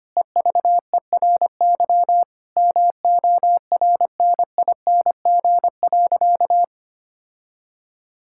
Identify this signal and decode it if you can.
Morse code, heard as a keyed tone, about 25 wpm, 700 Hz. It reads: EVERY MORNING.
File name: practice / 25wpm / Difficult Sentences with Hints - 25wpm